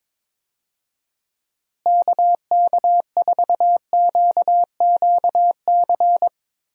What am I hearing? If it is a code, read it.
Morse code, heard as a keyed tone, about 22 wpm, 700 Hz. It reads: KK4QQC
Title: USA Call Signs - 6 Characters - Repeated 3x - 22wpm